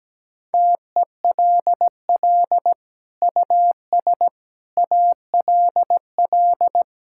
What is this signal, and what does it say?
Morse code, heard as a keyed tone, about 17 wpm, 700 Hz. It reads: TELL US ALL
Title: Sentences from Top 1000 Words - Review Part 5 - 17wpm